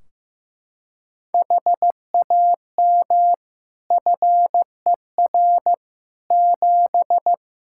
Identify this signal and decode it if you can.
Morse code, heard as a keyed tone, about 15 wpm, 700 Hz. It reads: HAM FER 7
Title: Practice QSO - Exchange 3 - More Info - 15wpm